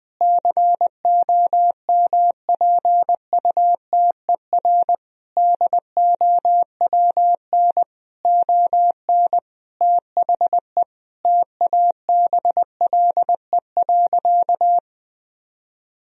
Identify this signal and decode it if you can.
Morse code, heard as a keyed tone, about 20 wpm, 700 Hz. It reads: COMPUTER DOWN ON THE TABLE.
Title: Sentences from Top 1000 Words - Review Part 3 - 20wpm